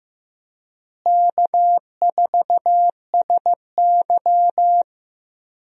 Morse code, heard as a keyed tone, about 15 wpm, 700 Hz. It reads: K4SY
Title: USA Call Signs - 4 Characters - Repeated 2x - 15wpm